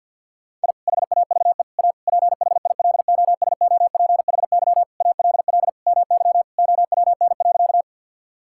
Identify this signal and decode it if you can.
Morse code, heard as a keyed tone, about 50 wpm, 700 Hz. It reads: I HAVE A PHILOSOPHY ALL MY OWN.
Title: Sentences from Top 1000 Words - Review Part 6 - 50wpm